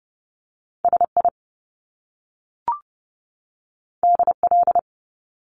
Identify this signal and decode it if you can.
Morse code, heard as a keyed tone, about 30 wpm, 700 Hz. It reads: SI DL
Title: Top Two Letter Combinations in Words 30wpm